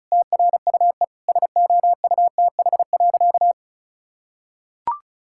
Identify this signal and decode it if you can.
Morse code, heard as a keyed tone, about 36 wpm, 700 Hz. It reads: TRUE SOUTH.
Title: Sentences from Top 1000 Words - Review Part 6 - 35wpm